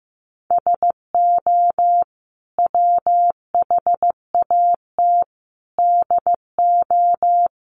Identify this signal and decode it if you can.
Morse code, heard as a keyed tone, about 15 wpm, 700 Hz. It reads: SO WHAT DO
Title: Sets of 5 Words Encore2 15wpm